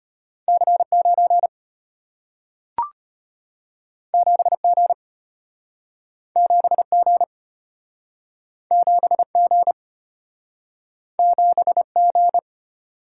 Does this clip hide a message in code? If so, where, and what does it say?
Yes: Morse code from 0.5 to 12.4 s, heard as a keyed tone, about 34 wpm, 700 Hz. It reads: C9 7G 7G 7G 7G